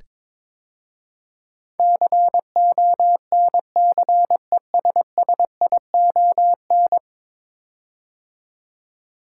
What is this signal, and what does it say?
Morse code, heard as a keyed tone, about 22 wpm, 700 Hz. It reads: CONCESSION
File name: practice / 22wpm / 10 Letter Words - First 3 Letters Given - 22wpm